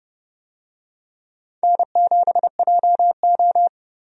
Morse code, heard as a keyed tone, about 31 wpm, 700 Hz. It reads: N7JO
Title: USA Call Signs - 4 Characters - 30wpm